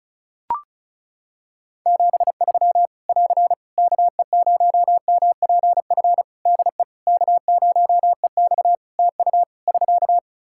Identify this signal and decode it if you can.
Morse code, heard as a keyed tone, about 36 wpm, 700 Hz. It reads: 73 <AR> KE0MPF DE K0EX TU <SK>
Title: Practice QSO - Exchange 4 - Ending - 35wpm